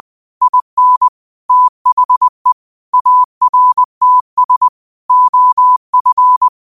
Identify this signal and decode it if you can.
Morse code, heard as a keyed tone, about 20 wpm, 1000 Hz. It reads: IN THE ARTS OF